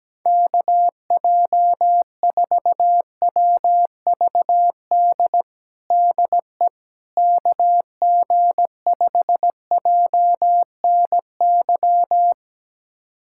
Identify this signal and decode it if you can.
Morse code, heard as a keyed tone, about 17 wpm, 700 Hz. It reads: KJ4WVD DE KG5JNY
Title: USA Callsign DE Callsign - 6 Characters - 17wpm